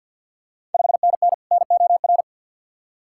Morse code, heard as a keyed tone, about 53 wpm, 690 Hz. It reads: 5NN NOR
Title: Ontario QSO Party Exchange 50wpm